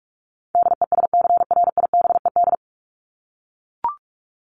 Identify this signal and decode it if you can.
Morse code, heard as a keyed tone, about 44 wpm, 700 Hz. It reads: DESCRIBED
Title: ED Suffix - Top 100 Words with Base Words 45wpm